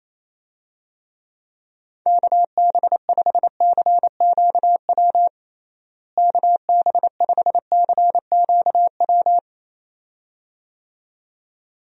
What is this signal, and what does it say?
Morse code, heard as a keyed tone, about 28 wpm, 700 Hz. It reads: KB5CQW KB5CQW
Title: USA Call Signs - 6 Characters - Repeated 2x - 28wpm